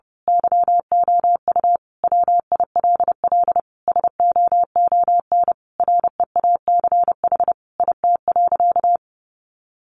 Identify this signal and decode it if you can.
Morse code, heard as a keyed tone, about 30 wpm, 700 Hz. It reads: YOU WILL SOON REACH IT.
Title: Sentences from Top 400 Words 30wpm